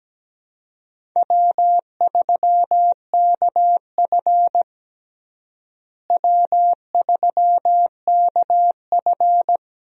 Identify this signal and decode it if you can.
Morse code, heard as a keyed tone, about 17 wpm, 700 Hz. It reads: W3KF W3KF
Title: USA Call Signs - 4 Characters - Repeated 3x - 17wpm